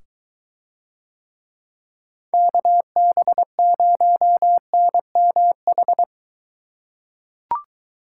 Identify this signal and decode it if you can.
Morse code, heard as a keyed tone, about 23 wpm, 700 Hz. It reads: KB0NMH